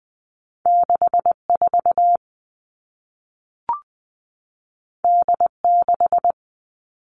Morse code, heard as a keyed tone, about 20 wpm, 700 Hz. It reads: 6 4 D 6
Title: All Cut Numbers - 2-Digit 20wpm